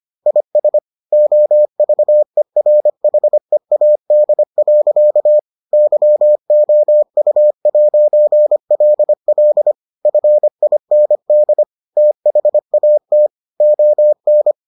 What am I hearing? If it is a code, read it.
Morse code, heard as a keyed tone, about 25 wpm, 600 Hz. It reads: IS OVERHEAD. YOU'LL FIND THAT ON